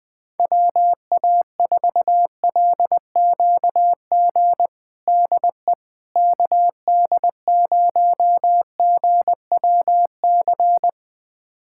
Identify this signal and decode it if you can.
Morse code, heard as a keyed tone, about 20 wpm, 700 Hz. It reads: WA4LQG DE KD0GWC
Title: USA Callsign DE Callsign - 6 Characters - 20wpm